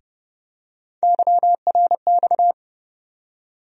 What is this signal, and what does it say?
Morse code, heard as a keyed tone, about 30 wpm, 700 Hz. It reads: YRX
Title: USA Call Sign Postfixes - 3 Letters 30wpm